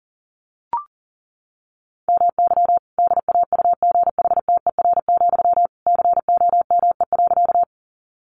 Morse code, heard as a keyed tone, about 40 wpm, 700 Hz. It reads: MY DAUGHTER, COME.
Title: Sentences from Top 1500 Words 40wpm